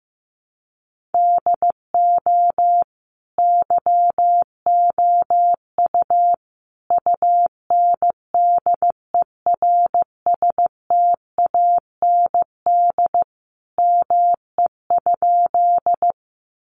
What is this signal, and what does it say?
Morse code, heard as a keyed tone, about 15 wpm, 700 Hz. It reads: DO YOU UNDERSTAND ME?
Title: Sentences from Top 600 Words 15wpm